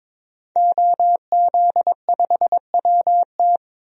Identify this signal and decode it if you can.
Morse code, heard as a keyed tone, about 22 wpm, 700 Hz. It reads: OZ5WT
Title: DXCC-100 Call Signs - 5 Characters - Repeated 3x - 22wpm